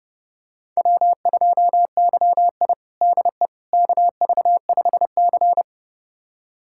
Morse code, heard as a keyed tone, about 32 wpm, 700 Hz. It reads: W2YI DE KV5C
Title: USA Callsign DE Callsign - 4 Characters - 30wpm